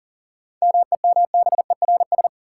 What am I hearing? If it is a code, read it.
Morse code, heard as a keyed tone, about 40 wpm, 700 Hz. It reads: MEMBERS
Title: Sets of 3 Words - New Word Drill - 40wpm